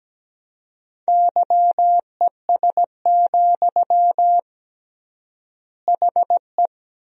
Morse code, heard as a keyed tone, about 17 wpm, 700 Hz. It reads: YES, HE
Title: 3x Word-Spacing - Sentences from Top 500 Words - 17wpm